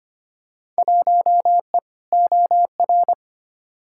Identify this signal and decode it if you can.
Morse code, heard as a keyed tone, about 25 wpm, 700 Hz. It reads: 1E OR
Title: Field Day Exchanges - 25wpm